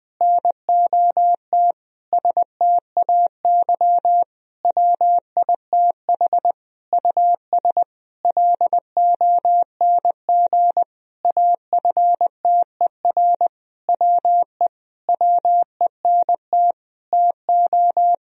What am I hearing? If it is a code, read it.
Morse code, heard as a keyed tone, about 20 wpm, 700 Hz. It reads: NOT STAY WITH US LONG AFTER WE WENT TO
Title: Sentences from Top 1000 Words - Review Part 2 - 20wpm